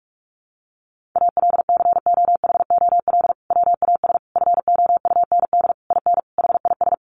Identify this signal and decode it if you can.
Morse code, heard as a keyed tone, about 44 wpm, 700 Hz. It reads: ALCOHOL WAS FOUND IN HIS
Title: Sentences from Top 1400 Words 45wpm